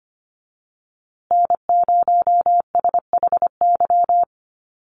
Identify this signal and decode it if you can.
Morse code, heard as a keyed tone, about 25 wpm, 700 Hz. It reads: N0SHY
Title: USA Call Signs Repeated 2x 25wpm